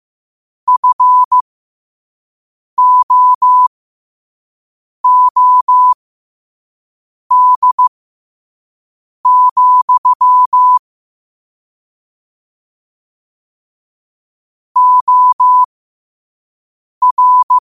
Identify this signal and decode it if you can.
Morse code, heard as a keyed tone, about 15 wpm, spaced as for 5 wpm, 1000 Hz. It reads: FOOD, OR
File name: qotdfiles / 5wpm